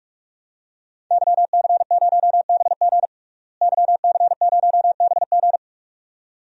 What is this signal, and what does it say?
Morse code, heard as a keyed tone, about 47 wpm, 700 Hz. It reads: YC0BG YC0BG
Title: DXCC-100 Call Signs - 5 Characters - Repeated 2x - 45wpm